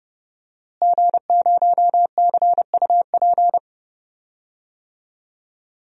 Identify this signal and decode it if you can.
Morse code, heard as a keyed tone, about 30 wpm, 700 Hz. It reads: G0CUP
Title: DXCC-100 Call Signs - 5 Characters - Repeated 3x - 30wpm